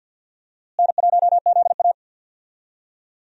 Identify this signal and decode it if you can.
Morse code, heard as a keyed tone, about 47 wpm, 700 Hz. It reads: N1ZA
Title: USA Call Signs - 4 Characters - Repeated 2x - 50wpm